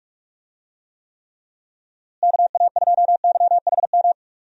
Morse code, heard as a keyed tone, about 41 wpm, 700 Hz. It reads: KA2YHM
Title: USA Call Signs - 6 Characters - Repeated 3x - 45wpm